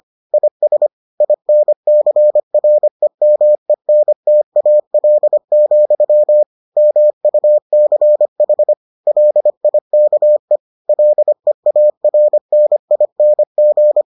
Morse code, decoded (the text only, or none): IS INCREMENTAL, MUCH LIKE LEARNING